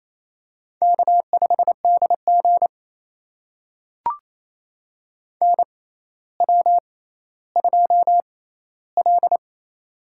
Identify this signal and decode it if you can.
Morse code, heard as a keyed tone, about 28 wpm, spaced as for 14 wpm, 700 Hz. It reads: K5DG NW2L
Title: USA Call Signs - 6x Character Spacing - 28wpm